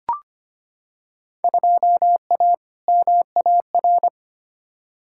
Morse code, heard as a keyed tone, about 25 wpm, 700 Hz. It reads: 2A MAR
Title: Field Day Exchanges - 25wpm